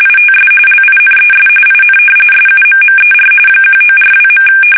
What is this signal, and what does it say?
RTTY, about 45.45 baud. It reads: ITS LINES IN SUMMER 1942.
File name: fsk850